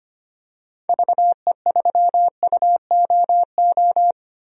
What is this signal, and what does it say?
Morse code, heard as a keyed tone, about 25 wpm, 700 Hz. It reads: VE3UOO
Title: DXCC-100 Call Signs - 6 Characters - Repeated 3x - 25wpm